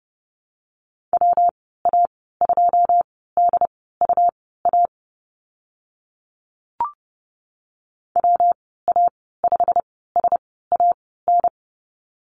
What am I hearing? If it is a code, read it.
Morse code, heard as a keyed tone, about 30 wpm, spaced as for 17 wpm, 700 Hz. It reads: WA2DUA WA5SAN
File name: Lesson 19g - Letter w - Call Signs 30wpm